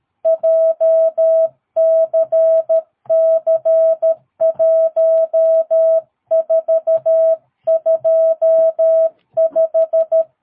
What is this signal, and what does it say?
Morse code, heard as a keyed tone, about 13 wpm, 650 Hz. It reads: JCC1425
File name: jh8cbh / 2022cw